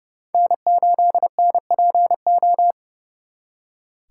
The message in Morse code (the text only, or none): N8NPO